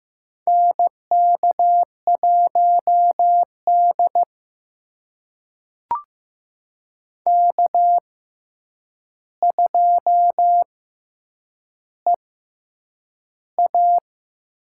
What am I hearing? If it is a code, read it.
Morse code, heard as a keyed tone, about 15 wpm, spaced as for 8 wpm, 700 Hz. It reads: NK1D K2EA
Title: USA Call Signs - 4 Characters - 6x Character Spacing - 15wpm